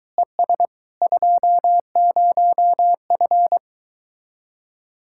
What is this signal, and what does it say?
Morse code, heard as a keyed tone, about 24 wpm, 700 Hz. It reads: ES 20F